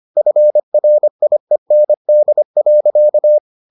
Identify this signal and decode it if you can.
Morse code, heard as a keyed tone, about 25 wpm, 600 Hz. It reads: FRIEND.